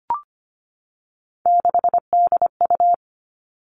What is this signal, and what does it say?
Morse code, heard as a keyed tone, about 25 wpm, 700 Hz. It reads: 6DU
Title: Three Challenging Letter-Number Combinations 25wpm